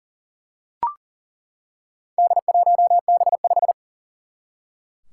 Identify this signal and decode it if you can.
Morse code, heard as a keyed tone, about 43 wpm, 700 Hz. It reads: D1B5
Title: Four Challenging Letter-Number Combinations 40wpm